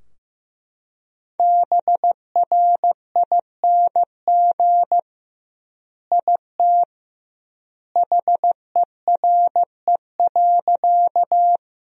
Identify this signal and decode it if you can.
Morse code, heard as a keyed tone, about 15 wpm, 700 Hz. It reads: BRING IT HERE.
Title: 2x Word-Spacing - Sentences from Top 500 Words - 15wpm